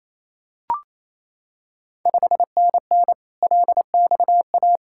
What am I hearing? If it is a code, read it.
Morse code, heard as a keyed tone, about 27 wpm, 690 Hz. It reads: 5NN LXA
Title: Ontario QSO Party Exchange 28wpm